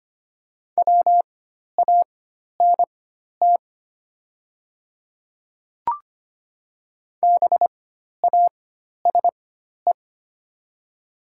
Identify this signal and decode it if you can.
Morse code, heard as a keyed tone, about 25 wpm, spaced as for 12 wpm, 700 Hz. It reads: WANT BASE
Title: 4x Character Spacing - Top 300 Words - 25wpm